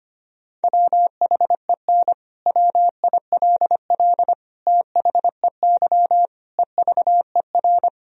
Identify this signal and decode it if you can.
Morse code, heard as a keyed tone, about 25 wpm, 700 Hz. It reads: WHEN WILL THEY EVER
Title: Sentences from Top 1000 Words - Review Part 4 - 25wpm